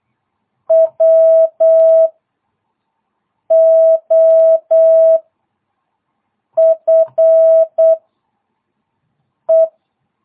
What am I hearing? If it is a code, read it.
Morse code, heard as a keyed tone, about 8 wpm, 660 Hz. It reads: W O F E